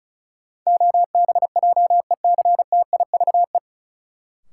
Morse code, heard as a keyed tone, about 34 wpm, 700 Hz. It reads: OBJECTIVE
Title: Sentences from Top 1200 Words - New Word Drill - 35wpm